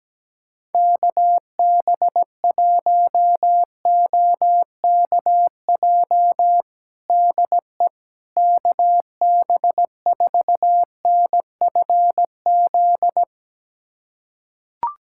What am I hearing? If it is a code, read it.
Morse code, heard as a keyed tone, about 17 wpm, 700 Hz. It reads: KB1OKJ DE KB4NFZ
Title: USA Callsign DE Callsign 17wpm